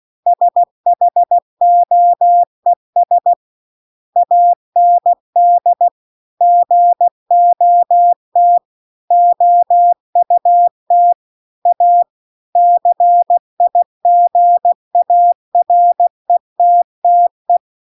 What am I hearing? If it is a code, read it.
Morse code, heard as a keyed tone, about 16 wpm, 700 Hz. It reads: SHOES AND GOT OUT A CIGARETTE